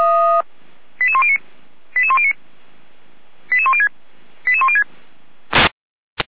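Five-tone selective call (ZVEI), heard as quick runs of five tones, tones of about 70 ms. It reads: 88109 88109 88106 88106
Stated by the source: ZVEI five-tone selective call 88102, received on an SDR.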